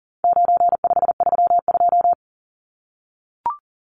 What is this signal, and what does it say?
Morse code, heard as a keyed tone, about 40 wpm, 700 Hz. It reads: 9532
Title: Numbers - 4-Digit 40wpm